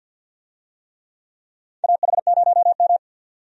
Morse code, heard as a keyed tone, about 51 wpm, 700 Hz. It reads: AH0M
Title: USA Call Signs - 4 Characters - 50wpm